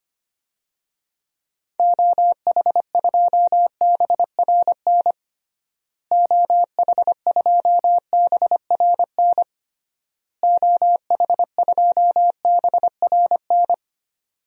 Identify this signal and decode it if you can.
Morse code, heard as a keyed tone, about 25 wpm, 700 Hz. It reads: OH2BRN OH2BRN OH2BRN
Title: DXCC-100 Call Signs - 6 Characters - Repeated 3x - 25wpm